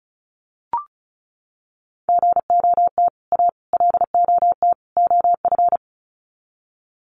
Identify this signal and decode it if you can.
Morse code, heard as a keyed tone, about 34 wpm, 700 Hz. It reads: GOT A LOT OF
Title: Sets of 4 Words 35wpm